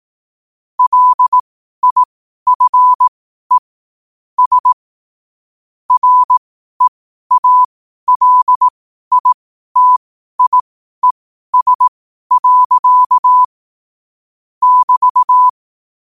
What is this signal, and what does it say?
Morse code, heard as a keyed tone, about 18 wpm, spaced as for 12 wpm, 1000 Hz. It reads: LIFE S REALITIES. <BT>